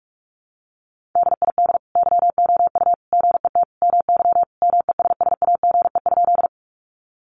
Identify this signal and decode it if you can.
Morse code, heard as a keyed tone, about 44 wpm, 700 Hz. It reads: DID YOU GET MY MESSAGE?
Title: Sentences from Top 1100 Words 45wpm